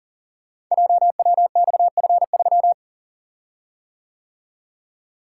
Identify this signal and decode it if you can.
Morse code, heard as a keyed tone, about 41 wpm, 690 Hz. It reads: JWXF3
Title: Code Groups - 5 Letter:Numbers - 40wpm